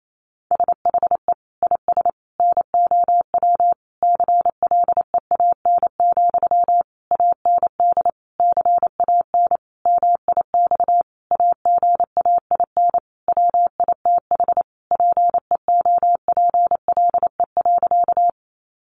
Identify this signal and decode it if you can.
Morse code, heard as a keyed tone, about 28 wpm, 700 Hz. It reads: SHE IS NOW CLEAN, AND CAN MIX AGAIN WITH PEOPLE.
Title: Sentences from Top 1100 Words 28wpm